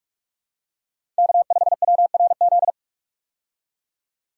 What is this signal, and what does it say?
Morse code, heard as a keyed tone, about 46 wpm, 700 Hz. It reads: K5WRZ
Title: USA Call Signs - 5 Characters - 45wpm